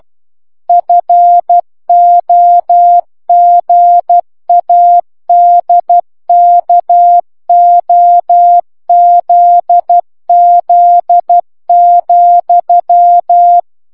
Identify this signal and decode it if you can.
Morse code, heard as a keyed tone, about 12 wpm, 700 Hz. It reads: FOGADKOZZ,